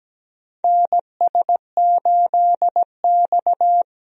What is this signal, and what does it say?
Morse code, heard as a keyed tone, about 17 wpm, 700 Hz. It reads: NS8X
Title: USA Call Signs - 4 Characters - Repeated 2x - 17wpm